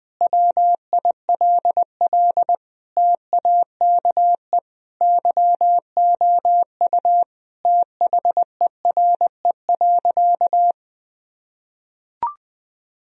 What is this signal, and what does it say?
Morse code, heard as a keyed tone, about 20 wpm, 700 Hz. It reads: WILL TAKE YOU THERE.
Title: Sentences from Top 1000 Words - Review Part 3 - 20wpm